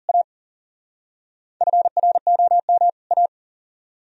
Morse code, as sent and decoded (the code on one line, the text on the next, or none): .- / ..-. .-. --- -- / .-
A FROM A